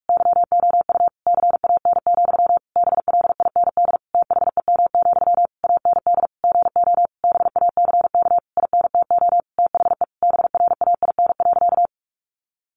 Morse code, heard as a keyed tone, about 44 wpm, 700 Hz. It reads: YOU CAN, BLIND THEM, AND GO BACK INTO THE BRAIN.
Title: A Sound Of Thunder 45wpm